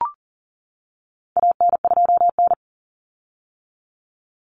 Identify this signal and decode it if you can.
Morse code, heard as a keyed tone, about 40 wpm, 700 Hz. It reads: AN2N
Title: Common Cut Numbers - 4-Digit 40wpm